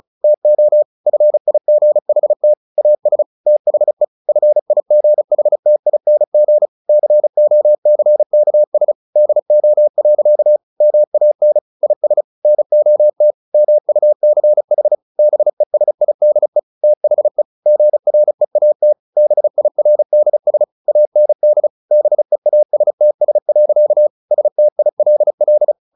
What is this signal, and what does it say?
Morse code, heard as a keyed tone, about 34 wpm, 600 Hz. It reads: TO FIGHT AS THE FIGHTING COCKS DO. MAN IS NOT MUCH BESIDE THE GREAT BIRDS AND BEASTS. STILL
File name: the old man and the sea / the old man and the sea 1 35wpm